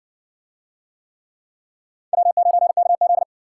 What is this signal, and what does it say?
Morse code, heard as a keyed tone, about 60 wpm, 700 Hz. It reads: W9KZ